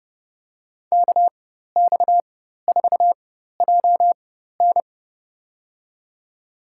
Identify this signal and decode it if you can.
Morse code, heard as a keyed tone, about 30 wpm, spaced as for 14 wpm, 690 Hz. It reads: KX4JN
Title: USA Call Signs - 4x Character Spacing - 30wpm